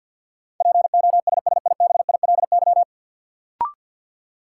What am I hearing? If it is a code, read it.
Morse code, heard as a keyed tone, about 50 wpm, 700 Hz. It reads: POSSIBILY
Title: Sentences from Top 1300 Words - New Word Drill - 50wpm